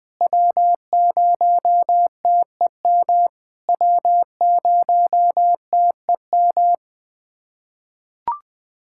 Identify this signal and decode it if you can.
Morse code, heard as a keyed tone, about 21 wpm, 700 Hz. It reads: W0TEM W0TEM
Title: SOTA QSO Practice 20wpm